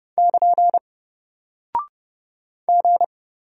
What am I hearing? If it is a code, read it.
Morse code, heard as a keyed tone, about 30 wpm, 700 Hz. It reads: <KN> G
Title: Single Character (Letter:Number:Punctuation:Prosign) Alt 30wpm